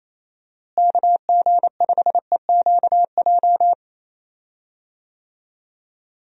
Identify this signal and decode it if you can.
Morse code, heard as a keyed tone, about 28 wpm, 700 Hz. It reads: KG5EQJ
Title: USA Call Signs - 6 Characters - Repeated 3x - 28wpm